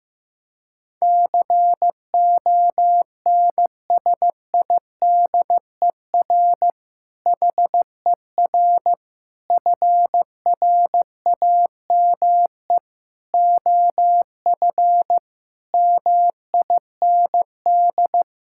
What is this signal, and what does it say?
Morse code, heard as a keyed tone, about 15 wpm, 700 Hz. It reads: CONSIDER HER FRAME OF MIND
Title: Sentences from Top 1000 Words - Review Part 6 - 15wpm